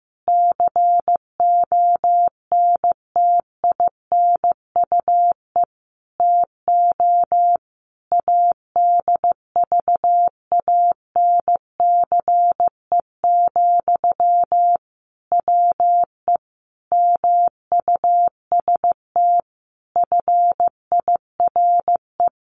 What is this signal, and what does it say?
Morse code, heard as a keyed tone, about 14 wpm, 700 Hz. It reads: CONTINUE TO ADVANCE, WE MUST FIRE
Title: Sentences from Top 1400 Words 15wpm